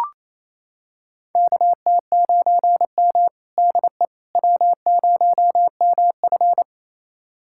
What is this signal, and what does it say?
Morse code, heard as a keyed tone, about 28 wpm, 700 Hz. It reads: KT9M DE W0MF
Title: USA Callsign DE Callsign - 4 Characters - 28wpm